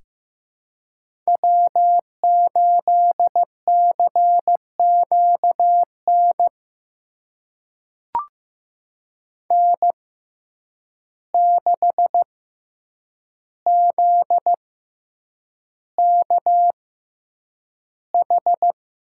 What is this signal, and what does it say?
Morse code, heard as a keyed tone, about 15 wpm, spaced as for 8 wpm, 700 Hz. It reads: W8CQN N6ZKH
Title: USA Call Signs - 5 Characters - 6x Character Spacing - 15wpm